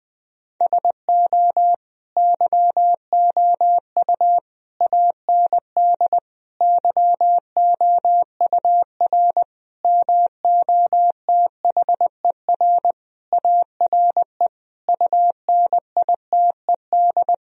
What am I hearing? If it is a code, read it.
Morse code, heard as a keyed tone, about 20 wpm, 700 Hz. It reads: SO YOU AND YOUR MOTHER ARE UNITED
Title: Sentences from Top 1000 Words - Review Part 4 - 20wpm